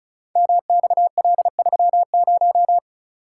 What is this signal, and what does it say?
Morse code, heard as a keyed tone, about 36 wpm, 700 Hz. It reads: MXL30
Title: Code Groups - 5 Letter:Numbers - 35wpm